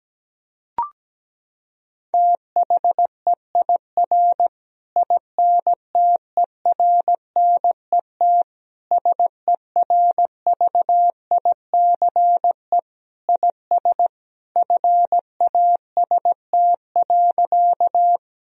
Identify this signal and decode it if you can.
Morse code, heard as a keyed tone, about 17 wpm, 700 Hz. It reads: THEIR INTERNET SERVICE IS FAST.
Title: Sentences from Top 1000 Words - Review Part 4 - 17wpm